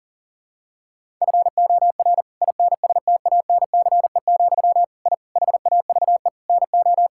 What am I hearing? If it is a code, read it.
Morse code, heard as a keyed tone, about 40 wpm, 700 Hz. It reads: FOR INSTANCE, I HAVE NO
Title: Sentences from Top 1000 Words - Review Part 5 - 40wpm